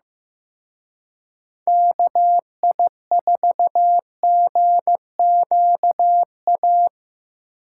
Morse code, heard as a keyed tone, about 18 wpm, 700 Hz. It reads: KI4GQA
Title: Speed Racing - USA 2x3 Call Signs 15wpm